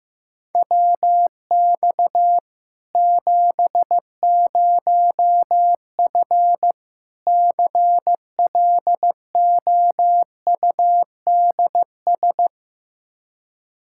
Morse code, heard as a keyed tone, about 15 wpm, 700 Hz. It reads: WX 70F CLOUDS
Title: K1USN SST QSO Practice 15wpm